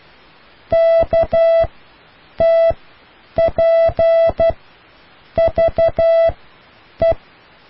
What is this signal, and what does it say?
Morse code, heard as a keyed tone, about 12 wpm, spaced as for 8 wpm, 660 Hz. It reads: KTPVE